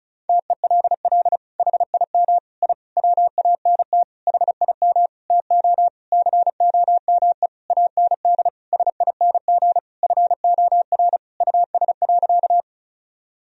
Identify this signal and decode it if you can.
Morse code, heard as a keyed tone, about 36 wpm, 700 Hz. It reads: TELL HIM I WANT HIM TO COME AND SING FOR US.
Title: Sentences from Top 1000 Words - Review Part 2 - 35wpm